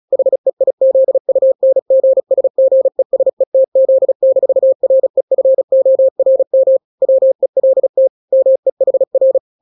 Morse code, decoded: HEIZUNGSGESETZ-REFORM WELT MEHR